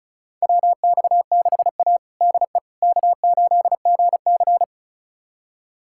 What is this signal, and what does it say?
Morse code, heard as a keyed tone, about 36 wpm, 700 Hz. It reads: WX6A DE K8GC
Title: USA Callsign DE Callsign - 4 Characters - 35wpm